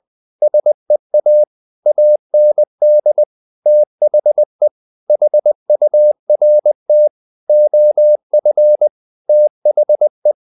Morse code, heard as a keyed tone, about 20 wpm, 600 Hz. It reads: SEA AND THE HURT OF THE
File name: the old man and the sea / the old man and the sea part 1 20wpm